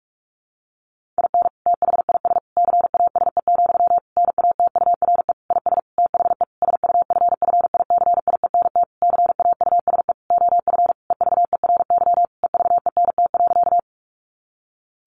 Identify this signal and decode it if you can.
Morse code, heard as a keyed tone, about 44 wpm, 700 Hz. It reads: IN THIS CASE, NATURE IS THE SUFFICIENT CAUSE OF EVERY EVENT.
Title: Sentences from Top 1400 Words 45wpm